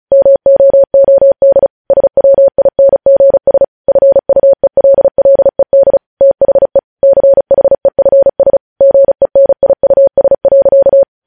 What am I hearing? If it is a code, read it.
Morse code, heard as a keyed tone, about 36 wpm, 550 Hz. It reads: MOOD SWINGS FUELLED THE CHEFS GENIUS.
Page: news headlines in morse code at 35 wpm